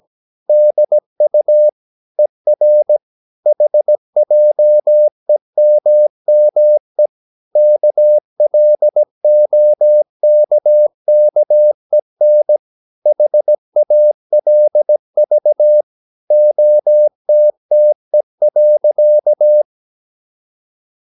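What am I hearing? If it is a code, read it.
Morse code, heard as a keyed tone, about 17 wpm, 600 Hz. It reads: DU ER HJEMME KLOKKEN HALV OTTE.
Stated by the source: Slow (QRS) Morse practice at 17 wpm, from CW med Gnister.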